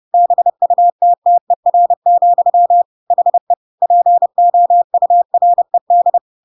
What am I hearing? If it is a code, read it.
Morse code, heard as a keyed tone, about 30 wpm, 700 Hz. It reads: BUTTER, HE POURED
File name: Pinocchio / Pinocchio ch5 30 30 ik1hge